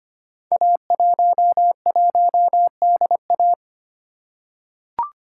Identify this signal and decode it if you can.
Morse code, heard as a keyed tone, about 25 wpm, 700 Hz. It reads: A11DA
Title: Code Groups - 5 Letter:Numbers - 25wpm